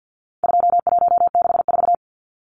Morse code, heard as a keyed tone, about 50 wpm, 700 Hz. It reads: 2164
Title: Numbers - 4-Digit 50wpm